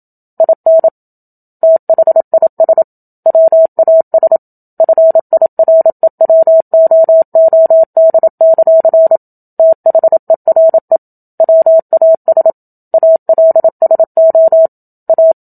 Morse code, heard as a keyed tone, about 27 wpm, 650 Hz. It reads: IN THIS WAS FIREWOOD; THERE WAS ALSO A
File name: Book Two / 01-Chapter 1 - 27 WPM